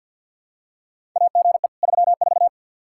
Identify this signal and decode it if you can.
Morse code, heard as a keyed tone, about 53 wpm, 700 Hz. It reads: AGE 34
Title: Practice QSO - Exchange 3 - More Info - 50wpm